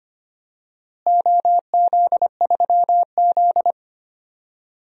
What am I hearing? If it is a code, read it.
Morse code, heard as a keyed tone, about 25 wpm, 700 Hz. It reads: OZ3Z
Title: DXCC-100 Call Signs - 4 Characters - Repeated 2x - 25wpm